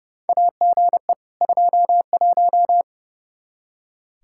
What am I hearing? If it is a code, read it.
Morse code, heard as a keyed tone, about 30 wpm, 700 Hz. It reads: AGE 21
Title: Practice QSO - Exchange 3 - More Info - 30wpm